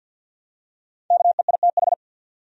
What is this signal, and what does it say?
Morse code, heard as a keyed tone, about 49 wpm, 700 Hz. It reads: KEITH
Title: USA Common Names - Max 5 Letters - 50wpm